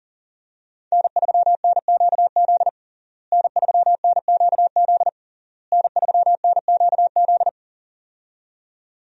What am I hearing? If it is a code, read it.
Morse code, heard as a keyed tone, about 40 wpm, 700 Hz. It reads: N3NQZ N3NQZ N3NQZ
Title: USA Call Signs - 5 Characters - Repeated 3x - 40wpm